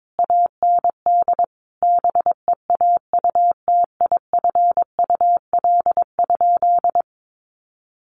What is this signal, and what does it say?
Morse code, heard as a keyed tone, about 22 wpm, 700 Hz. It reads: AND BEAUTIFUL?
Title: Sentences from Top 1200 Words 22wpm